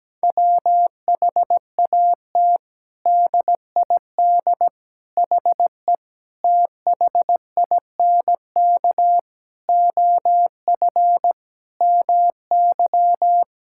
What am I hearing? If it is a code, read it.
Morse code, heard as a keyed tone, about 17 wpm, 700 Hz. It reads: WHAT DID HE THINK OF MY
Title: Sentences from Top 1000 Words - Review Part 6 - 17wpm